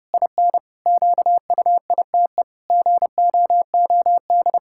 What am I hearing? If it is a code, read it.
Morse code, heard as a keyed tone, about 30 wpm, 700 Hz. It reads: IN QUITE GOOD
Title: Pangram Challenge v2 - 30wpm